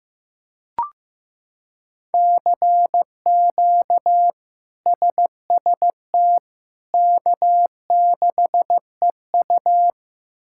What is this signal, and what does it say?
Morse code, heard as a keyed tone, about 16 wpm, 700 Hz. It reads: CQ SST K6EU
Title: K1USN SST QSO Practice - 15wpm